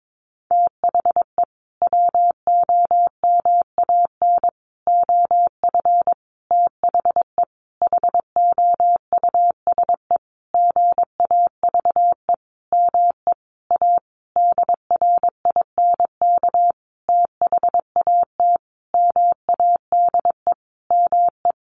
Morse code, heard as a keyed tone, about 22 wpm, 700 Hz. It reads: THE WOMAN OF THE HOUSE GAVE ME A DRINK THAT MADE ME
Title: Sentences from Top 1500 Words 22wpm